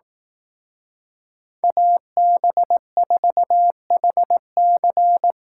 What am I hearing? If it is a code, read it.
Morse code, heard as a keyed tone, about 18 wpm, 700 Hz. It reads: AB4HC
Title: Speed Racing - USA 2x2 Call Signs 15wpm